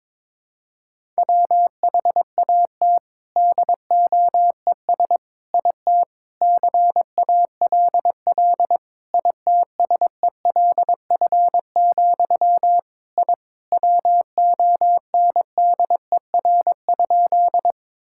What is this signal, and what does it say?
Morse code, heard as a keyed tone, about 22 wpm, 700 Hz. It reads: WHAT DOES IT CALL ITSELF, I WONDER?
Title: Sentences from Top 1000 Words - Review Part 5 - 22wpm